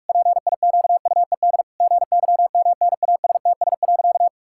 Morse code, heard as a keyed tone, about 46 wpm, 700 Hz. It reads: PIQUED GYMNASTS.
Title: Pangram Challenge v2 - 45wpm